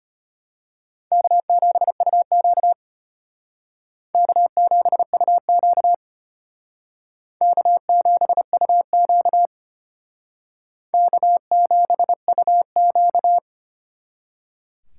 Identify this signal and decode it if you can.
Morse code, heard as a keyed tone, about 33 wpm, 700 Hz. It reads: K7UQ K7UQ K7UQ K7UQ